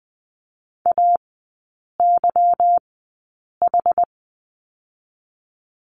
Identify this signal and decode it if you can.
Morse code, heard as a keyed tone, about 20 wpm, spaced as for 8 wpm, 700 Hz. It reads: AYH
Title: 152 Lesson 25e - Letter p - Three Characters 20wpm